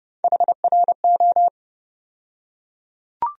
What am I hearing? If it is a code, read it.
Morse code, heard as a keyed tone, about 29 wpm, 700 Hz. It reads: HRO
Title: USA Call Sign Postfixes - 3 Letters 30wpm